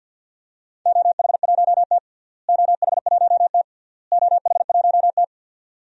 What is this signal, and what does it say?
Morse code, heard as a keyed tone, about 52 wpm, 700 Hz. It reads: OH1T OH1T OH1T
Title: DXCC-100 Call Signs - 4 Characters - Repeated 3x - 50wpm